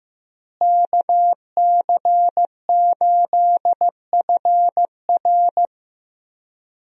Morse code, heard as a keyed tone, about 15 wpm, 700 Hz. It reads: KC8FR
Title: USA Call Signs - 5 Characters - Repeated 2x - 15wpm